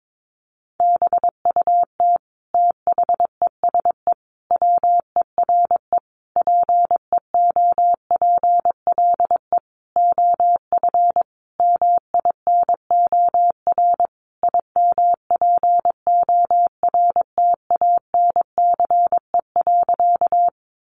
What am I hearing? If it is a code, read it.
Morse code, heard as a keyed tone, about 22 wpm, 700 Hz. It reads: BUT THESE WERE PEOPLE OF MINOR IMPORTANCE.
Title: Sentences from Top 1500 Words 22wpm